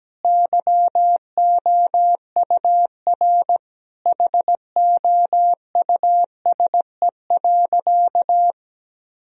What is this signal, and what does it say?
Morse code, heard as a keyed tone, about 17 wpm, 700 Hz. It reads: YOUR HOUSE.
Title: Sentences from Top 1000 Words - Review Part 2 - 17wpm